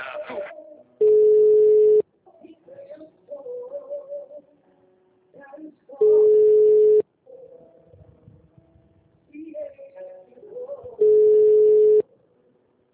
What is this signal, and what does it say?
Ringback tone (Europe, 425 Hz), with speech.
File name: out- -2010- - - .8